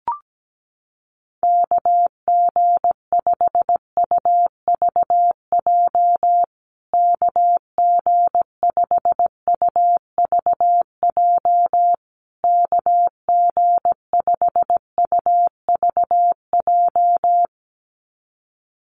Morse code, heard as a keyed tone, about 17 wpm, 700 Hz. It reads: KG5UVJ KG5UVJ KG5UVJ
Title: USA Call Signs Repeated 3x 17wpm